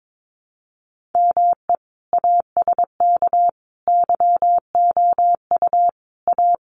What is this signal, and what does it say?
Morse code, heard as a keyed tone, about 22 wpm, 700 Hz. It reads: ME ASK YOU A
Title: Sets of 5 Words Encore2 22wpm